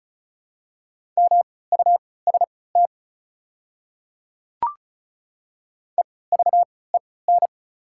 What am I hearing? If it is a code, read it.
Morse code, heard as a keyed tone, about 34 wpm, spaced as for 20 wpm, 700 Hz. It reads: MUST EVEN
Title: 3x Character Spacing - Top 300 Words - 35wpm